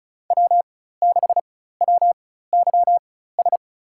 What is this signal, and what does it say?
Morse code, heard as a keyed tone, about 36 wpm, spaced as for 16 wpm, 700 Hz. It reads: W6WYS
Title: USA Call Signs - 5 Characters - 4x Character Spacing - 35wpm